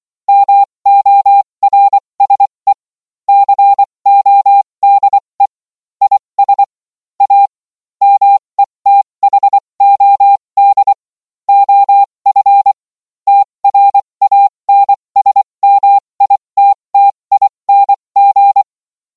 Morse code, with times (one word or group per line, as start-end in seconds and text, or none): 0.3-2.7 MORSE
3.3-5.5 CODE
6.0-6.6 IS
7.2-7.5 A
8.0-10.9 METHOD
11.5-12.7 OF
13.3-18.6 TRANSMITTING